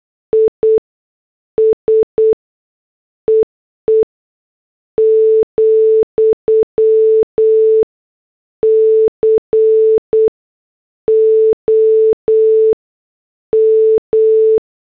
Morse code, heard as a keyed tone, about 8 wpm, spaced as for 6 wpm, 440 Hz. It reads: ISEE, COM
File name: morse / headlines